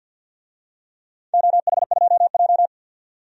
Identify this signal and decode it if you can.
Morse code, heard as a keyed tone, about 51 wpm, 700 Hz. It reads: OH2J
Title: DXCC-100 Call Signs - 4 Characters - 50wpm